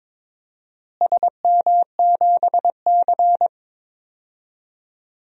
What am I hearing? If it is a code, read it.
Morse code, heard as a keyed tone, about 21 wpm, 700 Hz. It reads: SM7C
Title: DXCC-100 Call Signs - 4 Characters - Repeated 3x - 22wpm